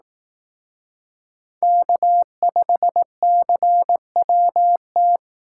Morse code, heard as a keyed tone, about 18 wpm, 700 Hz. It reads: K5CWT
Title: Speed Racing - USA Call Signs 15wpm